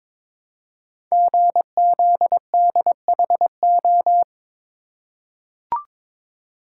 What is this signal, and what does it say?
Morse code, heard as a keyed tone, about 22 wpm, 700 Hz. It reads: GZDHO
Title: Code Groups - 5 Letters - 22wpm